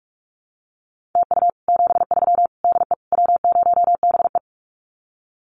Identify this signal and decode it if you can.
Morse code, heard as a keyed tone, about 44 wpm, 700 Hz. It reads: TU 73 DE W0BE
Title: Common QSO Phrases Encore 45wpm